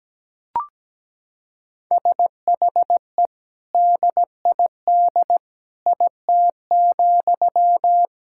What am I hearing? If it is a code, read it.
Morse code, heard as a keyed tone, about 17 wpm, 700 Hz. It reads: SHE DID IT,
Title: Sentences from Top 1000 Words - Review Part 6 - 17wpm